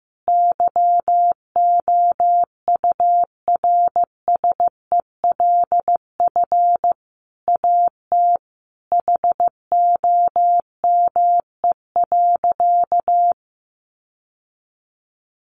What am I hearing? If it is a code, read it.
Morse code, heard as a keyed tone, about 14 wpm, 700 Hz. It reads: YOURSELF AT HOME.
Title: Sentences from Top 600 Words 15wpm